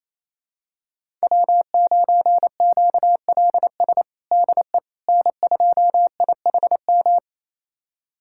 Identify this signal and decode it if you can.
Morse code, heard as a keyed tone, about 28 wpm, 700 Hz. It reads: W9QLS DE N2IHM
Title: USA Callsign DE Callsign - 5 Characters - 28wpm